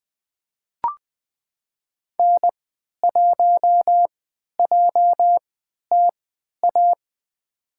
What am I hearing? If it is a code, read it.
Morse code, heard as a keyed tone, about 20 wpm, spaced as for 11 wpm, 700 Hz. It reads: N1JTA
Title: USA Call Signs - 5 Characters - 3x Character Spacing - 20wpm